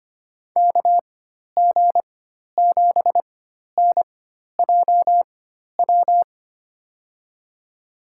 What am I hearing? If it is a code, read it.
Morse code, heard as a keyed tone, about 25 wpm, spaced as for 12 wpm, 700 Hz. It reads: KG7NJW
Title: USA Call Signs - 4x Character Spacing - 25wpm